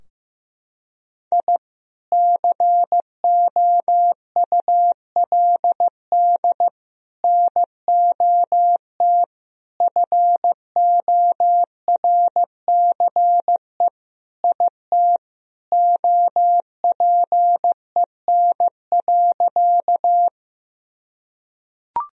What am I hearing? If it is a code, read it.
Morse code, heard as a keyed tone, about 15 wpm, 700 Hz. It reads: I COULD NOT FORCE IT OPEN.
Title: Sentences from Top 1000 Words - Review Part 3 - 15wpm